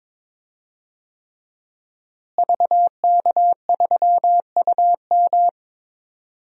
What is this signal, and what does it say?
Morse code, heard as a keyed tone, about 22 wpm, 700 Hz. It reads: VK3UM
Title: DXCC-100 Call Signs - 5 Characters - Repeated 3x - 22wpm